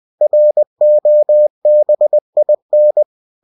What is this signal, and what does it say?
Morse code, heard as a keyed tone, about 20 wpm, 600 Hz. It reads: ROBIN
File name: cw-books / winnie-the-pooh / winnie-20wpm-page-0079